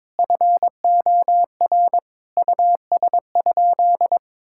Morse code, heard as a keyed tone, about 21 wpm, 700 Hz. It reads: FOR US?
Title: Sentences from Top 1000 Words - Review Part 5 - 22wpm